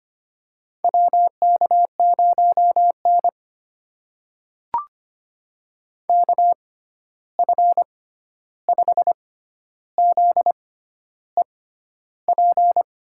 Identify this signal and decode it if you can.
Morse code, heard as a keyed tone, about 25 wpm, spaced as for 11 wpm, 700 Hz. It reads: WK0N KF5ZEP
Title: USA Call Signs - 6x Character Spacing - 25wpm